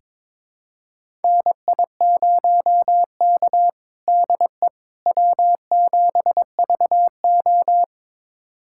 Morse code, heard as a keyed tone, about 22 wpm, 700 Hz. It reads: NI0K DE W7VO
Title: USA Callsign DE Callsign - 4 Characters - 22wpm